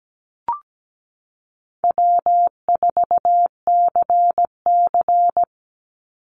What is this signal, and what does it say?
Morse code, heard as a keyed tone, about 17 wpm, 700 Hz. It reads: W4CC
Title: Call Signs - 17wpm